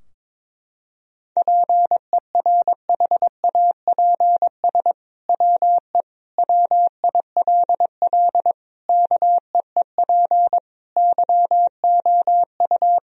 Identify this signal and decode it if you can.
Morse code, heard as a keyed tone, about 22 wpm, 700 Hz. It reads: PERHAPS WE WILL KEEP YOU
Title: Sentences from Top 1000 Words - Review Part 3 - 22wpm